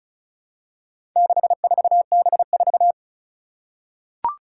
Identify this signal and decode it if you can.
Morse code, heard as a keyed tone, about 36 wpm, 700 Hz. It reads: 64B4
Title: Four Challenging Letter-Number Combinations 35wpm